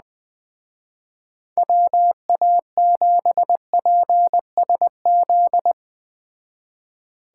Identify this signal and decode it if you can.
Morse code, heard as a keyed tone, about 20 wpm, 700 Hz. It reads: WA7PSZ
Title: Speed Racing - USA 2x3 Call Signs 17wpm